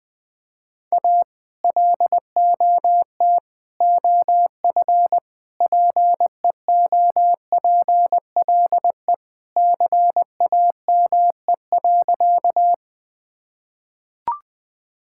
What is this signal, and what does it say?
Morse code, heard as a keyed tone, about 20 wpm, 700 Hz. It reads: A LOT OF PEOPLE CAME.
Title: Sentences with 3-Word Phrases - Spoken Beginning - 20wpm